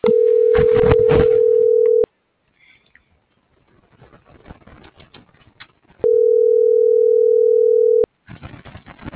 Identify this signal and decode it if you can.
Ringback tone (North America), 440 + 480 Hz.